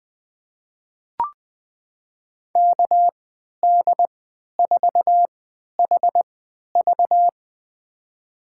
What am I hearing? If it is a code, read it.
Morse code, heard as a keyed tone, about 20 wpm, spaced as for 11 wpm, 700 Hz. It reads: KD4HV
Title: USA Call Signs - 5 Characters - 3x Character Spacing - 20wpm